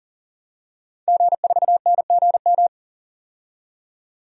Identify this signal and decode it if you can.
Morse code, heard as a keyed tone, about 40 wpm, 700 Hz. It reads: G4NGM
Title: DXCC-100 Call Signs - 5 Characters - 40wpm